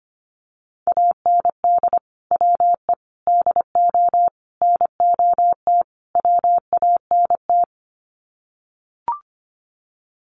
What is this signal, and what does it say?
Morse code, heard as a keyed tone, about 25 wpm, 700 Hz. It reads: AND WE DO NOT WANT
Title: Sets of 5 Words Encore2 25wpm